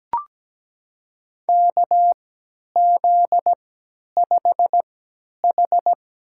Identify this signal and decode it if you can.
Morse code, heard as a keyed tone, about 17 wpm, spaced as for 10 wpm, 700 Hz. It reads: KZ5H
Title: USA Call Signs - 4 Characters - 3x Character Spacing - 17wpm